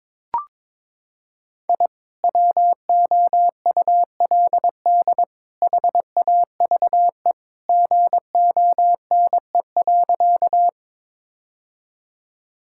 Morse code, heard as a keyed tone, about 22 wpm, 700 Hz. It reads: I WOULD HAVE GONE.
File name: practice / 22wpm / Sentences with 3-Word Phrases - 22wpm